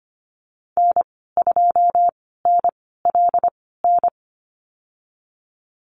Morse code, heard as a keyed tone, about 25 wpm, spaced as for 16 wpm, 700 Hz. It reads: N2NLN
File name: Lesson 15g - Number 5 - Call Signs 25wpm